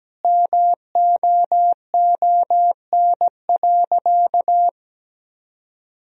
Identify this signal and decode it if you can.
Morse code, heard as a keyed tone, about 17 wpm, 700 Hz. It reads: MOON.
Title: Sentences from Top 1000 Words - Review Part 3 - 17wpm